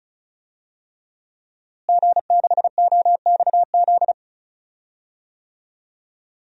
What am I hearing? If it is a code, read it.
Morse code, heard as a keyed tone, about 36 wpm, 700 Hz. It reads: G6OXZ
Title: DXCC-100 Call Signs - 5 Characters - Repeated 3x - 35wpm